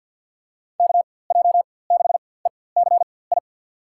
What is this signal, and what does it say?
Morse code, heard as a keyed tone, about 51 wpm, spaced as for 23 wpm, 700 Hz. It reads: KJ6ECI
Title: USA Call Signs - 6 Characters - 4x Character Spacing - 50wpm